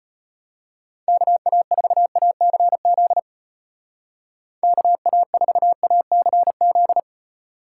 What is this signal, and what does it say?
Morse code, heard as a keyed tone, about 31 wpm, 700 Hz. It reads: KA4ACZ KA4ACZ
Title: Speed Racing - USA 2x3 Call Signs 25wpm